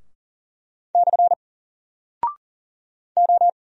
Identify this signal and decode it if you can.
Morse code, heard as a keyed tone, about 39 wpm, 700 Hz. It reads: / O
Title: Single Character (Letter:Number:Punctuation:Prosign) 40wpm